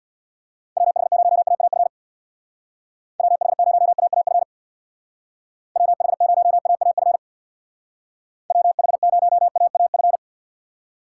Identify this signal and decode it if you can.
Morse code, heard as a keyed tone, about 60 wpm, 700 Hz. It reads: WH0AAF WH0AAF WH0AAF WH0AAF
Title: Speed Racing - USA Call Signs 50wpm